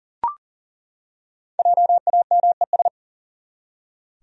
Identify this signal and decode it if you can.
Morse code, heard as a keyed tone, about 40 wpm, 700 Hz. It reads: JAMES
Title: USA Common Names - Max 5 Letters - 40wpm